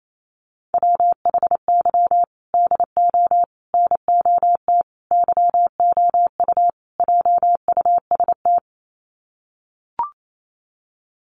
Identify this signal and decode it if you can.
Morse code, heard as a keyed tone, about 28 wpm, 700 Hz. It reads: WHY DO NOT YOU JUST
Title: Sets of 5 Words 28wpm